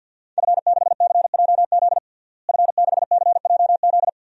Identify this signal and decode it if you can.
Morse code, heard as a keyed tone, about 50 wpm, 700 Hz. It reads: F6CJZ F6CJZ
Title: DXCC-100 Call Signs - 5 Characters - Repeated 2x - 50wpm